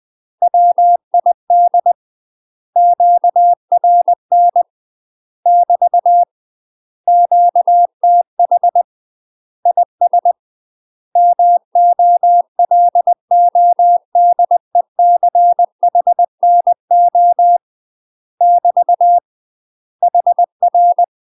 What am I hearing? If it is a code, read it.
Morse code, heard as a keyed tone, about 20 wpm, 700 Hz. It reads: WID QRN <BT> QTH IS MOLODECHNO <BT> HR
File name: UK MASTER QSOs TRAINING / 20WPM / UK MASTER QSOs 0047